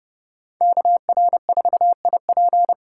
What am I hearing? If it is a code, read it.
Morse code, heard as a keyed tone, about 36 wpm, 700 Hz. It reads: KR4IP
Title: Speed Racing - USA 2x2 Call Signs 30wpm